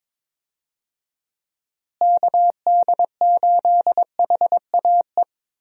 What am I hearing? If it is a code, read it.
Morse code, heard as a keyed tone, about 22 wpm, 700 Hz. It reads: KD8HAE
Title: USA Call Signs - 6 Characters - Repeated 3x - 22wpm